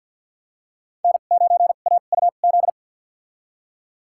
Morse code, heard as a keyed tone, about 53 wpm, spaced as for 37 wpm, 700 Hz. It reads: N9AUZ
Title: USA Call Signs - 5 Characters - 2x Character Spacing - 50wpm